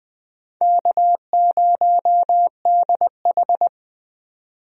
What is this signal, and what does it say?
Morse code, heard as a keyed tone, about 20 wpm, 700 Hz. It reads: K0DH
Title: USA Call Signs - 4 Characters - 20wpm